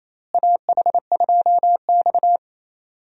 Morse code, heard as a keyed tone, about 28 wpm, 700 Hz. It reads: AH2X
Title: USA Call Signs - 4 Characters - Repeated 2x - 28wpm